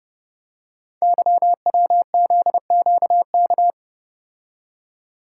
Morse code, heard as a keyed tone, about 29 wpm, 700 Hz. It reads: YWZQK
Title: Code Groups - 5 Letters - 30wpm